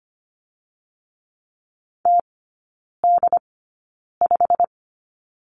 Morse code, heard as a keyed tone, about 25 wpm, spaced as for 9 wpm, 700 Hz. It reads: TD5
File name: Lesson 15e - Number 5 - Three Characters 25wpm